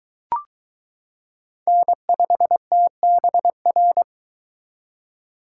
Morse code, heard as a keyed tone, about 22 wpm, 700 Hz. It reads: N5TBR